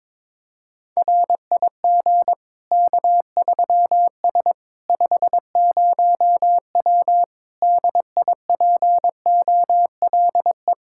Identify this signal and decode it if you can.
Morse code, heard as a keyed tone, about 22 wpm, 700 Hz. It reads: RIG K3S 50W DIPOLE
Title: K1USN SST QSO Practice 22wpm